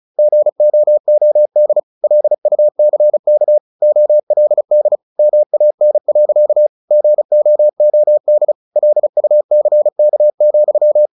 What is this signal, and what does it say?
Morse code, heard as a keyed tone, about 34 wpm, 600 Hz. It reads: GOOD LUCK OLD MAN. GOOD LUCK,
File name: the old man and the sea / the old man and the sea 1 35wpm